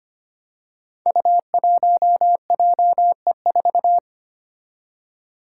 Morse code, heard as a keyed tone, about 25 wpm, 700 Hz. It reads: U1JE4
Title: Five Challenging Letter-Number Combinations 25wpm